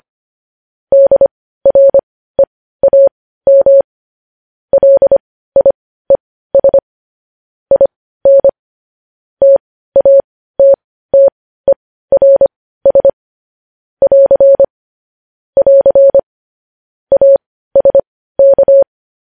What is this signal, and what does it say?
Morse code, heard as a keyed tone, about 25 wpm, spaced as for 15 wpm, 550 Hz. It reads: DREAM LIES IN TATTERS <AR> <AR> ASK
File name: podcast / News headlines-1x-25x15